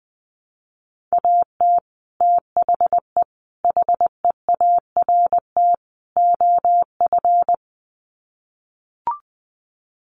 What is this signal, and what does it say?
Morse code, heard as a keyed tone, about 20 wpm, 700 Hz. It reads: AT THE HEART OF
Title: Sets of 4 Words 20wpm